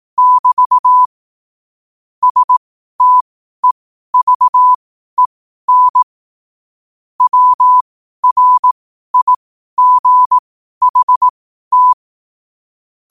Morse code, heard as a keyed tone, about 18 wpm, spaced as for 12 wpm, 1000 Hz. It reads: <BT> STEVEN WRIGHT
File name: qotdfiles / 12wpm